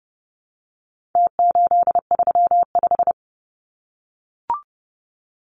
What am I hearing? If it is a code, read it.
Morse code, heard as a keyed tone, about 30 wpm, 700 Hz. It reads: T835
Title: Numbers - Contest Style - 4-Digit 30wpm